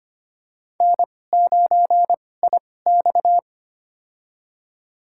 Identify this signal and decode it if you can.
Morse code, heard as a keyed tone, about 25 wpm, spaced as for 18 wpm, 700 Hz. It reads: N9IX
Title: USA Call Signs - 4 Characters - 2x Character Spacing - 25wpm